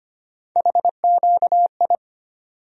Morse code, heard as a keyed tone, about 25 wpm, 700 Hz. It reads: HQI
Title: USA Call Sign Postfixes - 3 Letters 25wpm